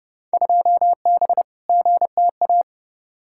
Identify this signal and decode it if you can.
Morse code, heard as a keyed tone, about 30 wpm, 700 Hz. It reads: 2B GTA
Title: Field Day Exchanges Encore - 30wpm